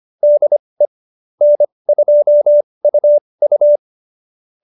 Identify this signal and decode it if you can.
Morse code, heard as a keyed tone, about 25 wpm, spaced as for 20 wpm, 590 Hz. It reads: DE N2UU